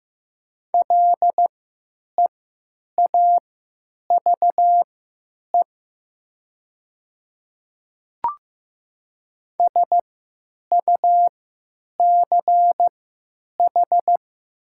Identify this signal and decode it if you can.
Morse code, heard as a keyed tone, about 15 wpm, spaced as for 9 wpm, 700 Hz. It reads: LEAVE SUCH
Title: 3x Character Spacing - Top 500 Words - 15wpm